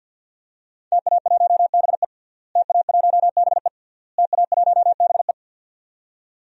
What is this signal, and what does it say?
Morse code, heard as a keyed tone, about 49 wpm, 700 Hz. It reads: TA1BE TA1BE TA1BE
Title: DXCC-100 Call Signs - 5 Characters - Repeated 3x - 50wpm